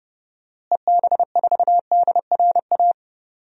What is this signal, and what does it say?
Morse code, heard as a keyed tone, about 30 wpm, 700 Hz. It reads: EB4DRA
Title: DXCC-100 Call Signs - 6 Characters - Repeated 3x - 30wpm